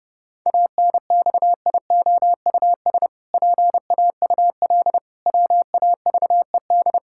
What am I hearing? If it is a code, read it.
Morse code, heard as a keyed tone, about 30 wpm, 700 Hz. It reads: ANXIOUS PAUL WAVED
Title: Pangram Challenge v2 - 30wpm